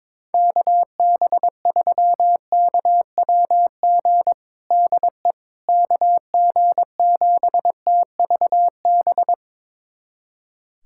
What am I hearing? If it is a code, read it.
Morse code, heard as a keyed tone, about 22 wpm, 700 Hz. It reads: KB3KWG DE KG7TVB
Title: USA Callsign DE Callsign 22wpm